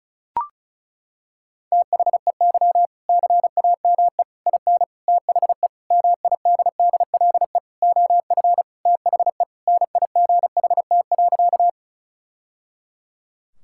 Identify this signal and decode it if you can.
Morse code, heard as a keyed tone, about 36 wpm, 700 Hz. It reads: THEY CAME IN THE MIDDLE OF THE NIGHT.
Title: Sentences from Top 1000 Words - Review Part 6 - 35wpm